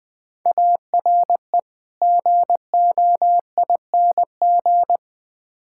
Morse code, heard as a keyed tone, about 20 wpm, 700 Hz. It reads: ARE GOING
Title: Build A Phrase - 5-Word Phrases - 20wpm